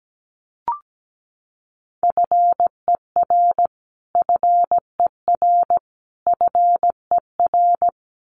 Morse code, heard as a keyed tone, about 17 wpm, 700 Hz. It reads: FER FER FER
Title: Abbreviations Repeated 3x - 17wpm